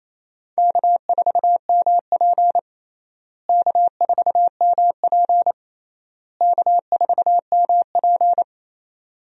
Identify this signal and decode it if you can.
Morse code, heard as a keyed tone, about 28 wpm, 700 Hz. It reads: K4MP K4MP K4MP
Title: USA Call Signs - 4 Characters - Repeated 3x - 28wpm